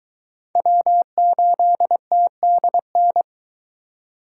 Morse code, heard as a keyed tone, about 21 wpm, 700 Hz. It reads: W8TDN